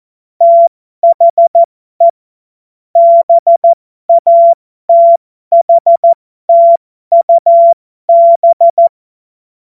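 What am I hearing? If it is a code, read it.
Morse code, heard as a keyed tone, about 14 wpm, spaced as for 12 wpm, 680 Hz. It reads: THE BATHTUB
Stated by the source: Morse code joke practice.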